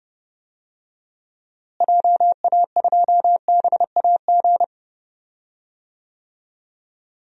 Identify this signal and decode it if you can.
Morse code, heard as a keyed tone, about 30 wpm, 700 Hz. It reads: JA2BAG
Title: DXCC-100 Call Signs - 6 Characters - Repeated 3x - 30wpm